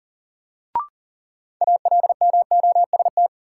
Morse code, heard as a keyed tone, about 40 wpm, 700 Hz. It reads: ALMOST
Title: Sentences from Top 700 Words - New Word Drill - 40wpm